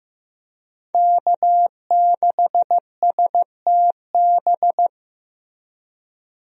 Morse code, heard as a keyed tone, about 15 wpm, 700 Hz. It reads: K6STB
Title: USA Call Signs - 5 Characters - Repeated 3x - 15wpm